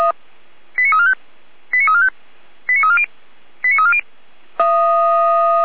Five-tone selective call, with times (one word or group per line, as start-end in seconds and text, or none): 0.8-1.1 89345
1.7-2.1 89345
2.7-3.1 89340
3.6-4.0 89340